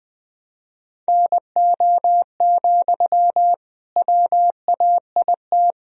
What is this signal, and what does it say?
Morse code, heard as a keyed tone, about 20 wpm, 700 Hz. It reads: NO, WAIT
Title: Sentences from Top 1000 Words - Review Part 1 - 20wpm